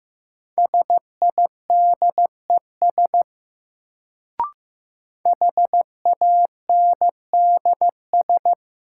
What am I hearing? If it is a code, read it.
Morse code, heard as a keyed tone, about 15 wpm, 700 Hz. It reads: SIDES HANDS
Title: Sets of 4 Words - New Word Drill - 15wpm